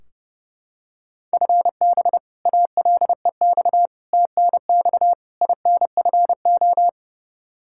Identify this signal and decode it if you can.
Morse code, heard as a keyed tone, about 30 wpm, 700 Hz. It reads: FB ALEX TNX INFO
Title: Practice QSO - Exchange 3 - More Info - 30wpm